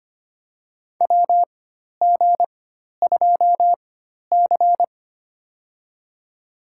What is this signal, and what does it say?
Morse code, heard as a keyed tone, about 25 wpm, spaced as for 12 wpm, 700 Hz. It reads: WG2C
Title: USA Call Signs - 4 Characters - 4x Character Spacing - 25wpm